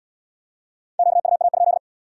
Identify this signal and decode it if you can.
Morse code, heard as a keyed tone, about 75 wpm, 700 Hz. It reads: QUA?